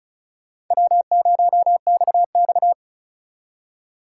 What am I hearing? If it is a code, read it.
Morse code, heard as a keyed tone, about 45 wpm, 700 Hz. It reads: W0XX